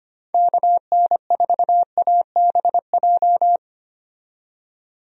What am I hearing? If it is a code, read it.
Morse code, heard as a keyed tone, about 25 wpm, 700 Hz. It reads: KN4ABJ
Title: USA Call Signs - 6 Characters - 25wpm